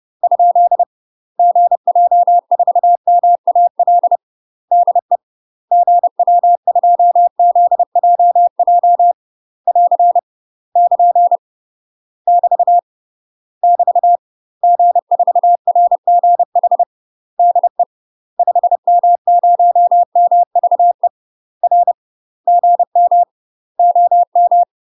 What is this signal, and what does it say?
Morse code, heard as a keyed tone, about 30 wpm, 700 Hz. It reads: ? GJ4MAL DE GW2ZJJ <AR> <KN> <BT> <BT> G4RGH DE 5M0MVE R GM OM
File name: UK MASTER QSOs TRAINING / 30WPM / UK MASTER QSOs 0022